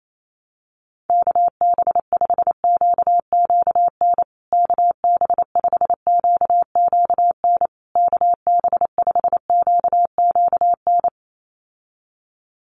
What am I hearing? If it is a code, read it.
Morse code, heard as a keyed tone, about 28 wpm, 700 Hz. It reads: KB5QQN KB5QQN KB5QQN
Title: USA Call Signs Repeated 3x 28wpm